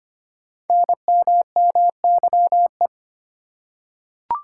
Morse code, heard as a keyed tone, about 25 wpm, 690 Hz. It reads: NMMYE
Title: Code Groups - 5 Letters - 25wpm